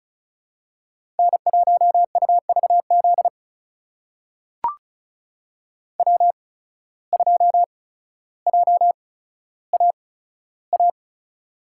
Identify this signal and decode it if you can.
Morse code, heard as a keyed tone, about 36 wpm, spaced as for 15 wpm, 700 Hz. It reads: N1UVZ W2JAA
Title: USA Call Signs - 5 Characters - 8x Character Spacing - 35wpm